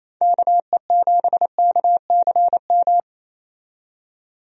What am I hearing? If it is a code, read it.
Morse code, heard as a keyed tone, about 28 wpm, 700 Hz. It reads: KE7KCM
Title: USA Call Signs - 6 Characters - 28wpm